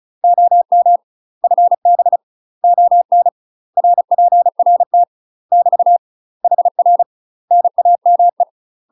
Morse code, heard as a keyed tone, about 36 wpm, 700 Hz. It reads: OM FB ON RPRT <BT> HR NAME
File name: UK MASTER QSOs TRAINING / 35WPM / UK MASTER QSOs 0014